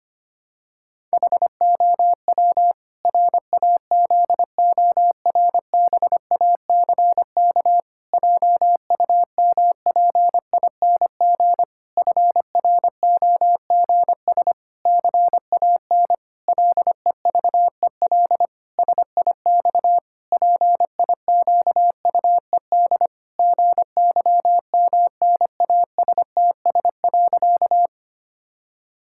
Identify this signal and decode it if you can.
Morse code, heard as a keyed tone, about 25 wpm, 700 Hz. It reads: HOW RAZORBACK JUMPING FROGS CAN LEVEL SIX PIQUED GYMNASTS.
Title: Pangram Challenge v2 - 25wpm